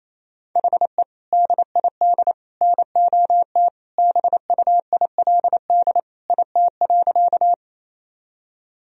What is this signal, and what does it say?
Morse code, heard as a keyed tone, about 27 wpm, 700 Hz. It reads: HE DID NOT BUILD IT.
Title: Sentences from Top 1000 Words - Review Part 1 - 28wpm